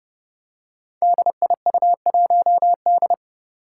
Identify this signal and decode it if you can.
Morse code, heard as a keyed tone, about 30 wpm, 700 Hz. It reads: DIU1D
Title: Five Challenging Letter-Number Combinations 30wpm